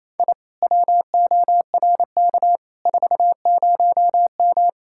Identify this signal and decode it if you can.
Morse code, heard as a keyed tone, about 35 wpm, 700 Hz. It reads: I WORK 40M
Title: Speed-Racing Common QSO Phrases - 28wpm